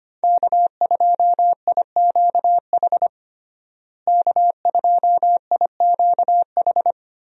Morse code, heard as a keyed tone, about 25 wpm, 700 Hz. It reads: K2IQH K2IQH
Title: USA Call Signs - 5 Characters - Repeated 3x - 25wpm